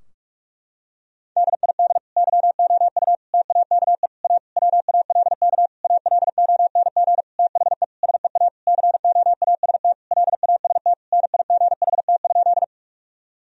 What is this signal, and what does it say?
Morse code, heard as a keyed tone, about 41 wpm, 700 Hz. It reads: DID YOU TAKE A WALK ALONG THE SEA COAST LAST NIGHT?
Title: Sentences from Top 1000 Words - Review Part 4 - 45wpm